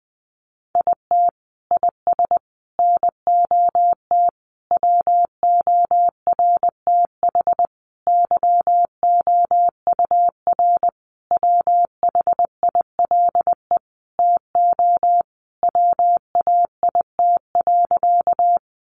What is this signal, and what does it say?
Morse code, heard as a keyed tone, about 20 wpm, 700 Hz. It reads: IT IS NOT WORTH YOUR WHILE TO WAIT.
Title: Sentences from Top 800 Words 20wpm